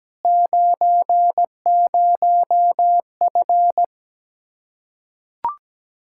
Morse code, heard as a keyed tone, about 17 wpm, 700 Hz. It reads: 90F
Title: Practice QSO - Exchange 2 - Info - 17wpm